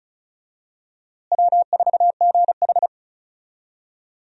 Morse code, heard as a keyed tone, about 34 wpm, 700 Hz. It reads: W4GH
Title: USA Call Signs - 4 Characters - Repeated 3x - 35wpm